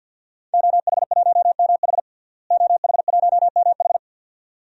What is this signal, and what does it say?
Morse code, heard as a keyed tone, about 47 wpm, 700 Hz. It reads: OH1MH OH1MH
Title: DXCC-100 Call Signs - 5 Characters - Repeated 2x - 50wpm